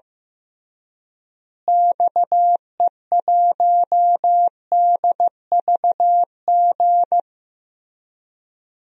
Morse code, heard as a keyed tone, about 14 wpm, 700 Hz. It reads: XE1DVG
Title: DXCC-100 Call Signs - 6 Characters - Repeated 3x - 15wpm